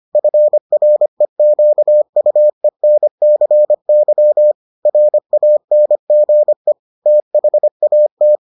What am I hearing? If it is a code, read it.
Morse code, heard as a keyed tone, about 25 wpm, 600 Hz. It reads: FREQUENCY RANGE THAT